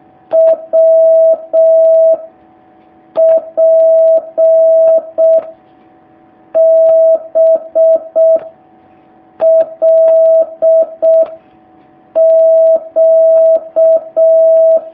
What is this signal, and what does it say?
Morse code, heard as a keyed tone, about 6 wpm, 650 Hz. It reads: W P B L Q
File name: jh8cbh / 2019cw